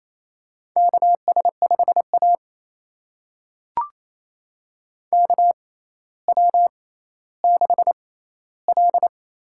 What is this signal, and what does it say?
Morse code, heard as a keyed tone, about 28 wpm, spaced as for 14 wpm, 700 Hz. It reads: KS5A KW6L
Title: USA Call Signs - 4 Characters - 6x Character Spacing - 28wpm